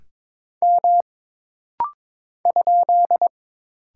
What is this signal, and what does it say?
Morse code, heard as a keyed tone, about 22 wpm, 700 Hz. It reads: M ?
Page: This rapid-fire timing is geared to push your ICR (Instant Character Recognition) skills.